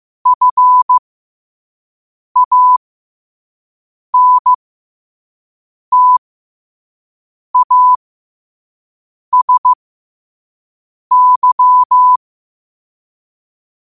Morse code, heard as a keyed tone, about 15 wpm, spaced as for 5 wpm, 1000 Hz. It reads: FANTASY
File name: 2024-10-07-5wpm